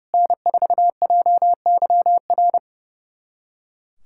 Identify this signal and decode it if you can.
Morse code, heard as a keyed tone, about 30 wpm, 700 Hz. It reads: N4JYR
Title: USA Callsign DE Callsign - 5 Characters - 30wpm